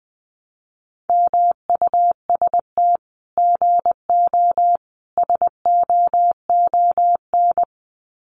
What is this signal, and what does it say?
Morse code, heard as a keyed tone, about 20 wpm, 700 Hz. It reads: MUST GO SOON
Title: Common QSO Phrases 20wpm